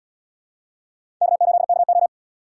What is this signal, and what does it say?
Morse code, heard as a keyed tone, about 70 wpm, 700 Hz. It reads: K8KO